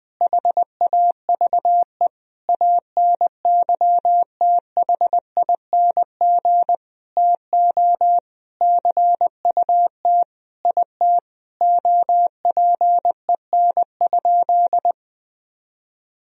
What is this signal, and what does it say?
Morse code, heard as a keyed tone, about 20 wpm, 700 Hz. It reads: HAVE ANYTHING TO CUT IT OPEN?
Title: Sentences from Top 1000 Words - Review Part 1 - 20wpm